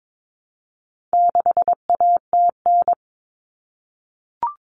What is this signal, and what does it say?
Morse code, heard as a keyed tone, about 22 wpm, 700 Hz. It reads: 6ATN
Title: All Cut Numbers - 4-Digit 22wpm